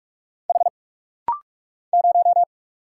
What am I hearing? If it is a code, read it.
Morse code, heard as a keyed tone, about 44 wpm, 700 Hz. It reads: H 0
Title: Single Letter-Number - Rapid-Fire 45wpm